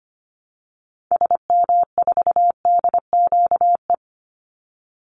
Morse code, heard as a keyed tone, about 25 wpm, 700 Hz. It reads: SM4DQE
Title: Call Signs Repeated 3x - 25wpm